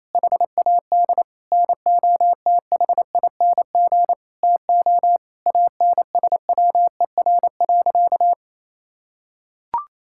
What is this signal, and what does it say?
Morse code, heard as a keyed tone, about 28 wpm, 700 Hz. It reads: HAD NOTHING TO ANSWER.
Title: Sentences from Top 1600 Words 28wpm